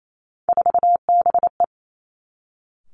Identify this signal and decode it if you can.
Morse code, heard as a keyed tone, about 28 wpm, 700 Hz. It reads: 4BE
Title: Three Challenging Letter-Number Combinations 28wpm